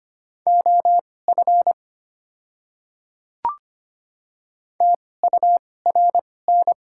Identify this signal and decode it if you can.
Morse code, heard as a keyed tone, about 25 wpm, spaced as for 18 wpm, 690 Hz. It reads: OF TURN
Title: 2x Character Spacing - Top 300 Words - 25wpm